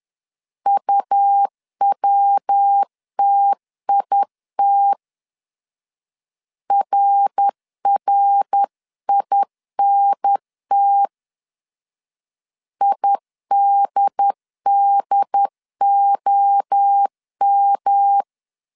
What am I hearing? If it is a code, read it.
Morse code, heard as a keyed tone, about 10 wpm, 800 Hz. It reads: UWTIT RRINT IDDOM
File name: CW / DET / 1 3 1500C S C 10wpm 3335